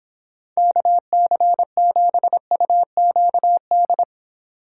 Morse code, heard as a keyed tone, about 26 wpm, 700 Hz. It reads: KC7UQD